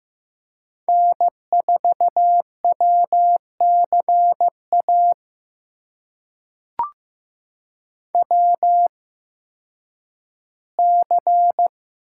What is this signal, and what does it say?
Morse code, heard as a keyed tone, about 15 wpm, spaced as for 10 wpm, 700 Hz. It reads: N4WCA WC
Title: USA Call Signs - 8x Character Spacing - 15wpm